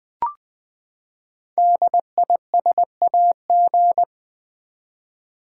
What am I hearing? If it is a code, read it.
Morse code, heard as a keyed tone, about 20 wpm, 700 Hz. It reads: DISAG
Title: Build A Word - 5 Letters At A Time - 12-Letter Words - 20wpm